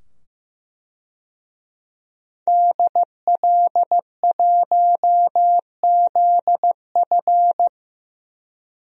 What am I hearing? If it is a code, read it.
Morse code, heard as a keyed tone, about 16 wpm, 700 Hz. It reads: DL1ZF
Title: DXCC-100 Call Signs - 5 Characters - 15wpm